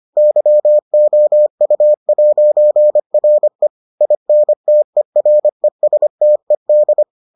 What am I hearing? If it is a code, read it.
Morse code, heard as a keyed tone, about 25 wpm, 600 Hz. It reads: YOU'RE INTERESTED